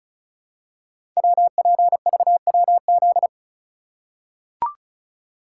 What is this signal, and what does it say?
Morse code, heard as a keyed tone, about 36 wpm, 700 Hz. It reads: WPVWZ
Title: Code Groups - 5 Letters - 35wpm